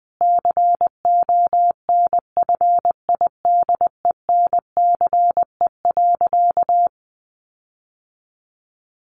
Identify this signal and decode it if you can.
Morse code, heard as a keyed tone, about 20 wpm, 700 Hz. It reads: CONFIDENCE.
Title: Sentences from Top 1500 Words 20wpm